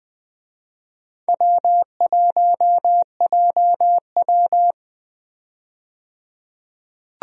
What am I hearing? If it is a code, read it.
Morse code, heard as a keyed tone, about 20 wpm, 700 Hz. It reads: W1JW
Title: USA Call Signs - 4 Characters - 20wpm